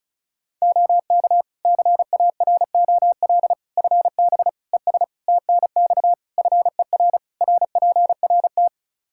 Morse code, heard as a keyed tone, about 34 wpm, 700 Hz. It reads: OK CAROL FB ES TNX FER RPRT
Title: K1USN SST QSO Practice 35wpm